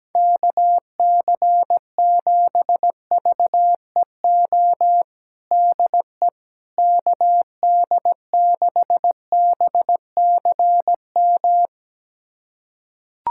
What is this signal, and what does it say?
Morse code, heard as a keyed tone, about 17 wpm, 700 Hz. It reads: KC7VEO DE KD6BCM
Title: USA Callsign DE Callsign - 6 Characters - 17wpm